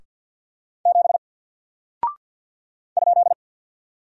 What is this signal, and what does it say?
Morse code, heard as a keyed tone, about 50 wpm, 700 Hz. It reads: 7 ?
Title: Single Character (Letter:Number:Punctuation:Prosign) 50wpm